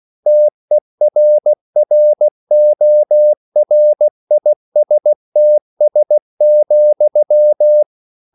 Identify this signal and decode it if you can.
Morse code, heard as a keyed tone, about 16 wpm, 600 Hz. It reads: TERRORISTS,